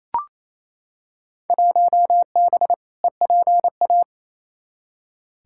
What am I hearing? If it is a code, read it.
Morse code, heard as a keyed tone, about 28 wpm, 700 Hz. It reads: 1B EPA
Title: Field Day Exchanges Encore - 28wpm